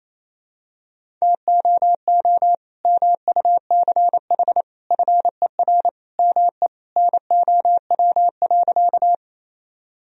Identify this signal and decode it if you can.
Morse code, heard as a keyed tone, about 28 wpm, 700 Hz. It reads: TOO MUCH FER ME NOW.
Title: Sentences from Top 200 Words and Abbreviations 28wpm